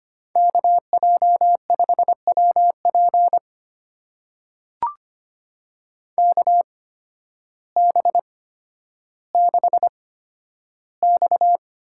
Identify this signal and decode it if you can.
Morse code, heard as a keyed tone, about 25 wpm, spaced as for 12 wpm, 700 Hz. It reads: KJ5WP KB6X
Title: USA Call Signs - 5 Characters - 8x Character Spacing - 25wpm